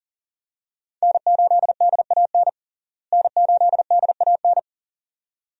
Morse code, heard as a keyed tone, about 40 wpm, 700 Hz. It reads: N8DAN N8DAN
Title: USA Call Signs - 5 Characters - Repeated 2x - 40wpm